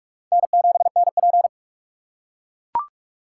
Morse code, heard as a keyed tone, about 43 wpm, 700 Hz. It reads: N7NP
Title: USA Call Signs - 4 Characters - 6x Character Spacing - 45wpm